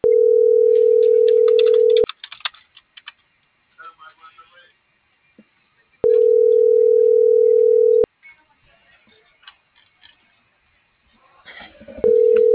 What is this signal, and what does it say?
Ringback tone (North America), 440 + 480 Hz.